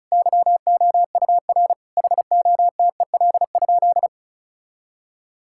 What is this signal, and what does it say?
Morse code, heard as a keyed tone, about 34 wpm, 700 Hz. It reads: YOUR HOTEL?
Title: Sentences from Top 1600 Words 35wpm